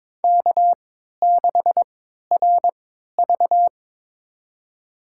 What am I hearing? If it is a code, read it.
Morse code, heard as a keyed tone, about 22 wpm, spaced as for 12 wpm, 700 Hz. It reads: K6RV
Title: USA Call Signs - 4 Characters - 3x Character Spacing - 22wpm